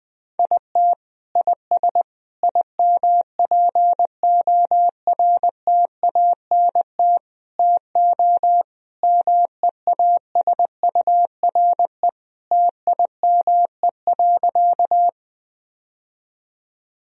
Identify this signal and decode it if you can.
Morse code, heard as a keyed tone, about 20 wpm, 700 Hz. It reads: IT IS IMPORTANT TO MEASURE TIME.
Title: Sentences from Top 1000 Words - Review Part 2 - 20wpm